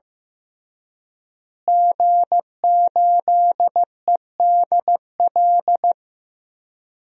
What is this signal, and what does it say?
Morse code, heard as a keyed tone, about 15 wpm, 700 Hz. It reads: G8EDL
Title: DXCC-100 Call Signs - 5 Characters - Repeated 2x - 15wpm